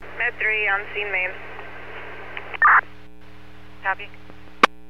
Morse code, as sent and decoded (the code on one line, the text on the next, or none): none